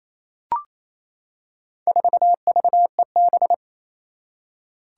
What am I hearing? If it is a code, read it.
Morse code, heard as a keyed tone, about 28 wpm, 700 Hz. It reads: 4VEB
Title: Four Challenging Letter-Number Combinations 28wpm